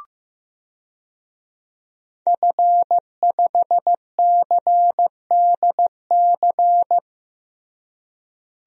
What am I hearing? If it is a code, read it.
Morse code, heard as a keyed tone, about 15 wpm, 700 Hz. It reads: F5CDC
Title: DXCC-100 Call Signs - 5 Characters - Repeated 2x - 15wpm